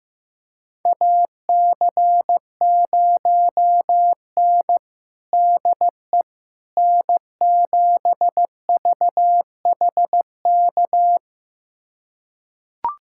Morse code, heard as a keyed tone, about 15 wpm, 700 Hz. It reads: AC0N DE N7VHK
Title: USA Callsign DE Callsign 15wpm